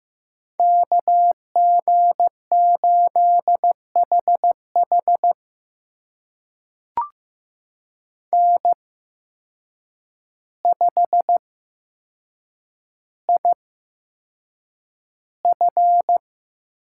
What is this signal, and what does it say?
Morse code, heard as a keyed tone, about 15 wpm, spaced as for 7 wpm, 700 Hz. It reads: KG8HH N5IF
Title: USA Call Signs - 5 Characters - 8x Character Spacing - 15wpm